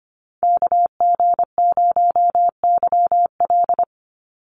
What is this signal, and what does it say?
Morse code, heard as a keyed tone, about 25 wpm, 700 Hz. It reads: KG0YL
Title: Call Signs Repeated 2x - 25wpm